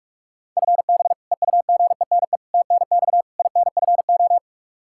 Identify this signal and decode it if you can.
Morse code, heard as a keyed tone, about 46 wpm, 700 Hz. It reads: FB EUGENE TNX INFO
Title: Practice QSO - Exchange 3 - More Info - 45wpm